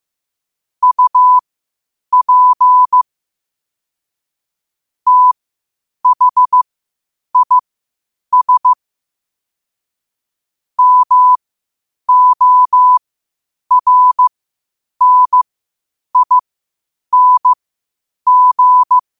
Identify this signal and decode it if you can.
Morse code, heard as a keyed tone, about 15 wpm, spaced as for 8 wpm, 1000 Hz. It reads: UP THIS MORNING
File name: qotdfiles / 8wpm